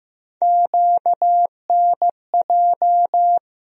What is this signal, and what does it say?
Morse code, heard as a keyed tone, about 15 wpm, 700 Hz. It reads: QNJ
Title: USA Call Sign Postfixes - 3 Letters 15wpm